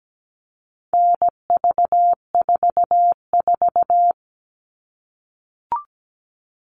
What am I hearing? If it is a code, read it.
Morse code, heard as a keyed tone, about 17 wpm, 700 Hz. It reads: NV44
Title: All Cut Numbers - 4-Digit 17wpm